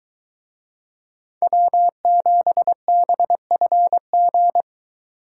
Morse code, heard as a keyed tone, about 23 wpm, 700 Hz. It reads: W7BFG